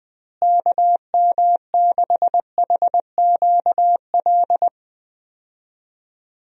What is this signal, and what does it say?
Morse code, heard as a keyed tone, about 20 wpm, 700 Hz. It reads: KM6HQL
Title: USA Call Signs - 6 Characters - Repeated 3x - 20wpm